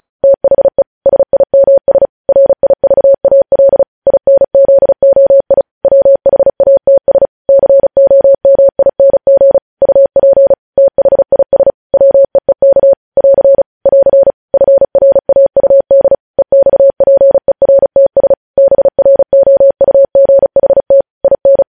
Morse code, heard as a keyed tone, about 36 wpm, 550 Hz. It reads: THE SIMS RIVAL INZOI WHATS COMING UP THIS WEEK <AR> <AR> FRAUD EXPERTS BROUGHT IN
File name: podcast / News headlines-1x-35x35